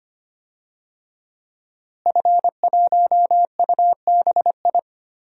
Morse code, heard as a keyed tone, about 25 wpm, 700 Hz. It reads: F1UBI
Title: DXCC-100 Call Signs - 5 Characters - Repeated 3x - 25wpm